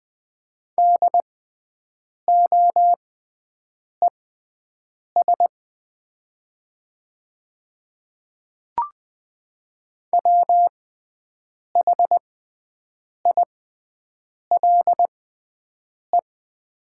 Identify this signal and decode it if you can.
Morse code, heard as a keyed tone, about 20 wpm, spaced as for 7 wpm, 700 Hz. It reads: DOES WHILE
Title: 6x Character Spacing - Top 500 Words - 20wpm